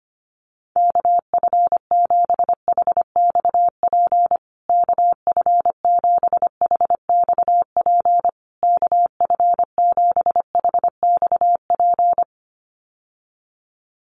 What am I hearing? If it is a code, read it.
Morse code, heard as a keyed tone, about 25 wpm, 700 Hz. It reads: KF7HXP KF7HXP KF7HXP
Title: USA Call Signs Repeated 3x 25wpm